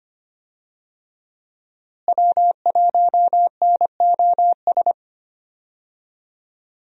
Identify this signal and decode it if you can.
Morse code, heard as a keyed tone, about 25 wpm, 700 Hz. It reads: W1NOS
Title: USA Call Signs - 5 Characters - Repeated 2x - 25wpm